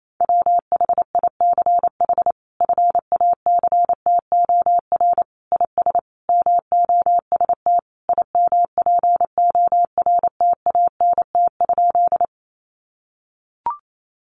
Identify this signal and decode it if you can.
Morse code, heard as a keyed tone, about 28 wpm, 700 Hz. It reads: WHICH FACTOR IS MOST IMPORTANT?
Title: Sentences from Top 1300 Words 28wpm